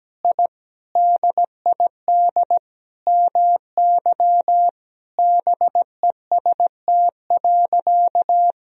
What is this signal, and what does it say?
Morse code, heard as a keyed tone, about 17 wpm, 700 Hz. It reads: I DID MY BEST.
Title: Sentences from Top 1000 Words - Review Part 4 - 17wpm